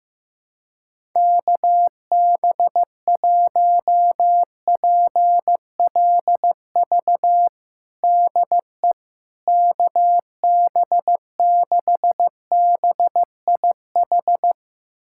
Morse code, heard as a keyed tone, about 15 wpm, 700 Hz. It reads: KB1PLV DE KB6BIH
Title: USA Callsign DE Callsign - 6 Characters - 15wpm